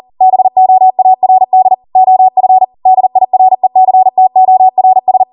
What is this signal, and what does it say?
Morse code, heard as a keyed tone, about 40 wpm, 750 Hz. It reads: BOARD OF DIRECTORS